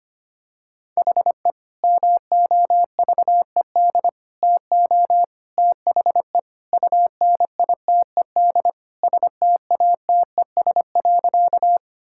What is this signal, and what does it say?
Morse code, heard as a keyed tone, about 25 wpm, 700 Hz. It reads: HE MOVED TO THE UNITED STATES.
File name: practice / 25wpm / Sentences with 3-Word Phrases - 25wpm